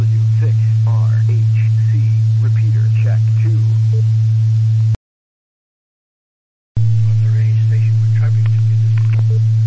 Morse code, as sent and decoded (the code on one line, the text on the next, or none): none